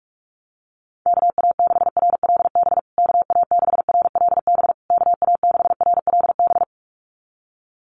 Morse code, heard as a keyed tone, about 44 wpm, 700 Hz. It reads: KA6RLB KA6RLB KA6RLB
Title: USA Call Signs Repeated 3x 45wpm